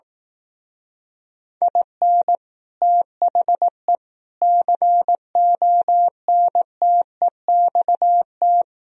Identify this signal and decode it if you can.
Morse code, heard as a keyed tone, about 18 wpm, 700 Hz. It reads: IN THE CONTEXT
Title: Speed-Racing - Sets of 4 Words 15wpm